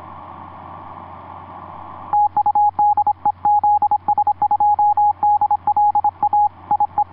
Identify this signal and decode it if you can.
Morse code, heard as a keyed tone, about 26 wpm, 830 Hz. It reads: TU DE ZS2DL A IE
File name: zs2dl ea2ok etc 12m cw 07mar2010 1637